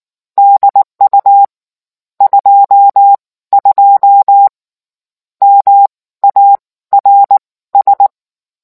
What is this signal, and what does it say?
Morse code, heard as a keyed tone, about 19 wpm, spaced as for 14 wpm, 800 Hz. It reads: DU 22 MARS